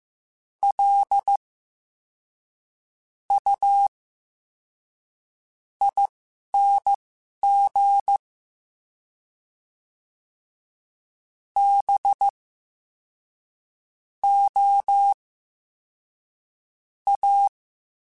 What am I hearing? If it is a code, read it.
Morse code, heard as a keyed tone, about 15 wpm, spaced as for 11 wpm, 800 Hz. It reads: L U ING B O A
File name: Special INGs-mp3s / Special INGs-mp3s / SlowToFaster ing @10